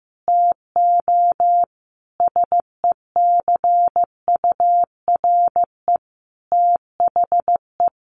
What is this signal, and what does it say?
Morse code, heard as a keyed tone, about 15 wpm, 700 Hz. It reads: TO SECURE THE
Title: Sentences from Top 1200 Words 15wpm